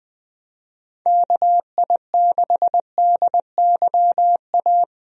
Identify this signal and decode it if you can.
Morse code, heard as a keyed tone, about 20 wpm, 700 Hz. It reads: KI6DYA
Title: USA Call Signs - 6 Characters - 20wpm